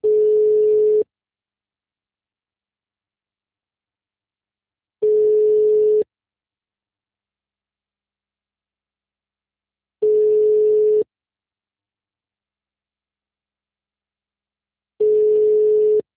Ringback tone (Europe, 425 Hz).
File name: out- -2002- - - .148